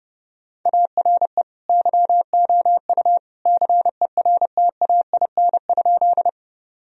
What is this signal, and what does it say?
Morse code, heard as a keyed tone, about 30 wpm, 700 Hz. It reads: ARE YOU CERTAIN?
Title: Sentences from Top 1000 Words - Review Part 2 - 30wpm